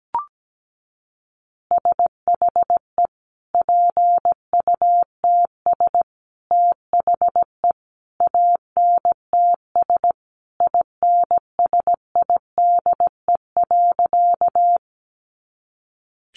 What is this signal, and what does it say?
Morse code, heard as a keyed tone, about 17 wpm, 700 Hz. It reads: SHE PUTS THE ANTS INSIDE.
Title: Additional Easy Phrases 17wpm